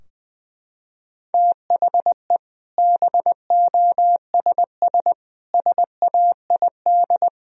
Morse code, heard as a keyed tone, about 20 wpm, 700 Hz. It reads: THE BOSS SAID
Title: Sentences from Top 1000 Words - Review Part 4 - 20wpm